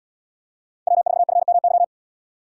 Morse code, heard as a keyed tone, about 77 wpm, 700 Hz. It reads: W4FMY